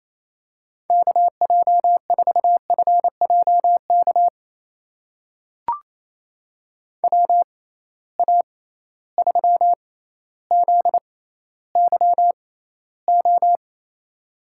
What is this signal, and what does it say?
Morse code, heard as a keyed tone, about 28 wpm, spaced as for 14 wpm, 700 Hz. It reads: KJ4FJK WA3ZYO
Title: USA Call Signs - 6 Characters - 6x Character Spacing - 28wpm